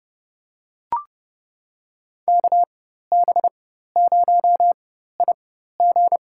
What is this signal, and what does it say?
Morse code, heard as a keyed tone, about 29 wpm, spaced as for 14 wpm, 700 Hz. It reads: KB0IG
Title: USA Call Signs - 4x Character Spacing - 30wpm